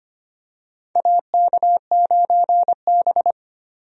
Morse code, heard as a keyed tone, about 25 wpm, 700 Hz. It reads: AK9B
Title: USA Call Signs - 4 Characters - 25wpm